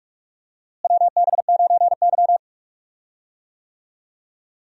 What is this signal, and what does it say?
Morse code, heard as a keyed tone, about 46 wpm, 700 Hz. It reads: WB9Y
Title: USA Call Signs - 4 Characters - Repeated 3x - 45wpm